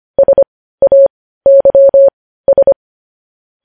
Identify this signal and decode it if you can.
Morse code, heard as a keyed tone, about 25 wpm, spaced as for 15 wpm, 550 Hz. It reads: SAYS
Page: news headlines in morse code at 15 wpm